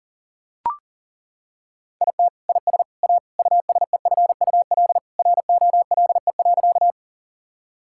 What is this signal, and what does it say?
Morse code, heard as a keyed tone, about 40 wpm, 700 Hz. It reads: IT IS A USEFUL ROLE.
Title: Sentences from Top 1000 Words - Review Part 6 - 40wpm